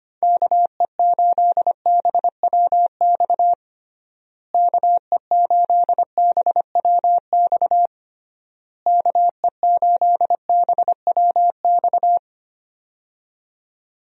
Morse code, heard as a keyed tone, about 25 wpm, 700 Hz. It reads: KE8BWX KE8BWX KE8BWX
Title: USA Call Signs - 6 Characters - Repeated 3x - 25wpm